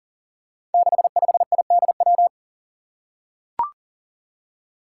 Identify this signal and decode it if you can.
Morse code, heard as a keyed tone, about 40 wpm, 700 Hz. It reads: 65IDW
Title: Five Challenging Letter-Number Combinations 40wpm